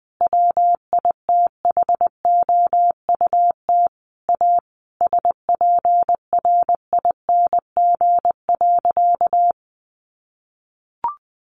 Morse code, heard as a keyed tone, about 20 wpm, 700 Hz. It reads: WITHOUT A SPRING.
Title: Sentences from Top 1200 Words 20wpm